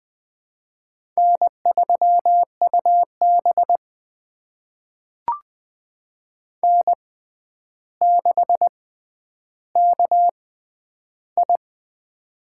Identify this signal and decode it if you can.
Morse code, heard as a keyed tone, about 20 wpm, spaced as for 10 wpm, 700 Hz. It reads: N3UB N6KI
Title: USA Call Signs - 4 Characters - 6x Character Spacing - 20wpm